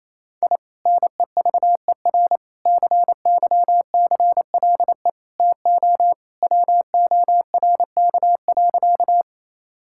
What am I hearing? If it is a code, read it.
Morse code, heard as a keyed tone, about 28 wpm, 700 Hz. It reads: I NEVER CYCLE TO WORK.
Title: Sentences from Top 1000 Words - Review Part 6 - 28wpm